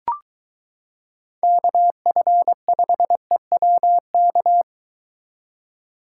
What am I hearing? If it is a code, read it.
Morse code, heard as a keyed tone, about 23 wpm, 700 Hz. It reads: KF5EWK